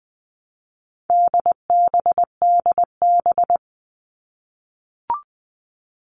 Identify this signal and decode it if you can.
Morse code, heard as a keyed tone, about 20 wpm, 700 Hz. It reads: DBDB
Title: All Cut Numbers - 4-Digit 20wpm